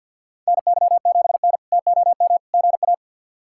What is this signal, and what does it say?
Morse code, heard as a keyed tone, about 50 wpm, 700 Hz. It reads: NY7N TOM GA
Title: K1USN SST QSO Practice - 50wpm